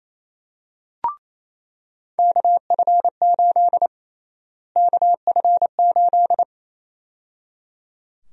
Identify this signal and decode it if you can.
Morse code, heard as a keyed tone, about 28 wpm, 700 Hz. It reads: KF8 KF8
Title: USA Call Sign Prefixes - Repeated 2x - 28wpm